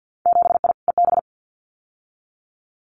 Morse code, heard as a keyed tone, about 50 wpm, 700 Hz. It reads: 7I EB
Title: Winter Field Day Exchanges Encore 50wpm